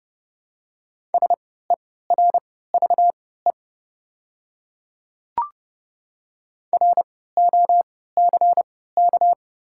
Morse code, heard as a keyed tone, about 29 wpm, spaced as for 17 wpm, 700 Hz. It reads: SERVE ROCK
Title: 3x Character Spacing - Top 500 Words - 30wpm